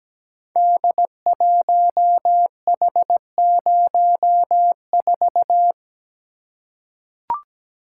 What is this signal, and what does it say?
Morse code, heard as a keyed tone, about 17 wpm, 700 Hz. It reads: D1H04
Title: Code Groups - 5 Letter:Numbers - 17wpm